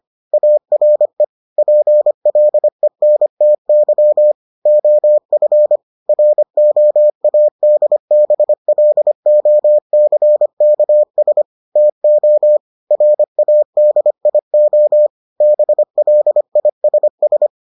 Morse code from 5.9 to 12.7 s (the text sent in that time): ROADBLOCKS TO